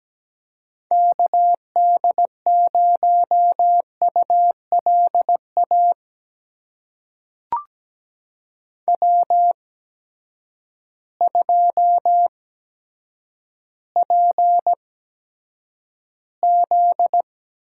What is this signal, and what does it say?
Morse code, heard as a keyed tone, about 17 wpm, spaced as for 8 wpm, 700 Hz. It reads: KD0ULA W2PZ
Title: USA Call Signs - 8x Character Spacing - 17wpm